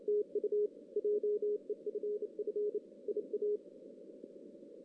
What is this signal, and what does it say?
Morse code, heard as a keyed tone, about 29 wpm, 420 Hz. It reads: TU JEFF IA
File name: 2016 JAN CW NAQP / 2016 Jan naqp k5ot - 0-6